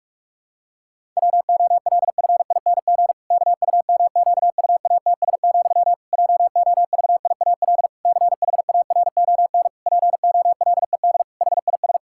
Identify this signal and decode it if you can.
Morse code, heard as a keyed tone, about 46 wpm, 700 Hz. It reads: WOLFING KUMQUATS, JOVIAL CHARON POLED HIS
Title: Pangram Challenge v2 - 45wpm